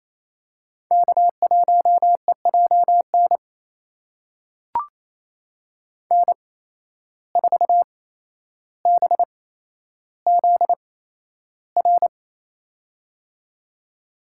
Morse code, heard as a keyed tone, about 28 wpm, spaced as for 12 wpm, 700 Hz. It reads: K1EJN N4BZR
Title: USA Call Signs - 5 Characters - 8x Character Spacing - 28wpm